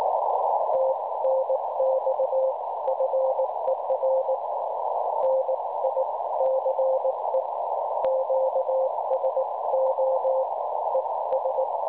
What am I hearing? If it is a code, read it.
Morse code, heard as a keyed tone, about 19 wpm, spaced as for 16 wpm, 530 Hz, with noise. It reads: TNX FER NICE QSO ES
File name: cwronde / websdr recording .6kHz